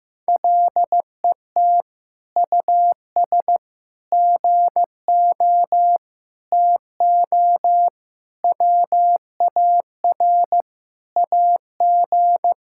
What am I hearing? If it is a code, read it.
Morse code, heard as a keyed tone, about 16 wpm, 700 Hz. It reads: LET US GO TO WAR AG
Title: Sentences from Top 1000 Words - Review Part 5 - 15wpm